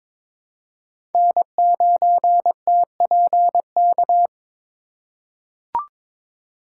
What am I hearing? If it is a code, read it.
Morse code, heard as a keyed tone, about 22 wpm, 700 Hz. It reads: N9TPK
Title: USA Call Signs - 5 Characters - 3x Character Spacing - 22wpm